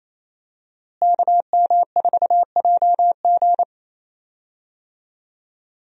Morse code, heard as a keyed tone, about 27 wpm, 700 Hz. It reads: KM4JG
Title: USA Call Signs - 5 Characters - Repeated 2x - 28wpm